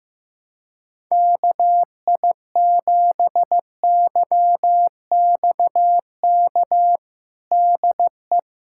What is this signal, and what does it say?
Morse code, heard as a keyed tone, about 15 wpm, 700 Hz. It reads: KI7YXK DE
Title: USA Callsign DE Callsign 15wpm